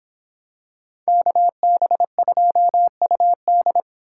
Morse code, heard as a keyed tone, about 26 wpm, 700 Hz. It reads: KB2UD